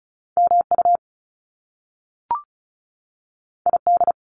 Morse code, heard as a keyed tone, about 36 wpm, 700 Hz. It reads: MU ID
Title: Top Two Letter Combinations in Words 35wpm